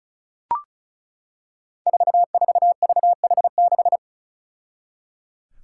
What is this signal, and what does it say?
Morse code, heard as a keyed tone, about 36 wpm, 700 Hz. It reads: 44VH6
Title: Five Challenging Letter-Number Combinations 35wpm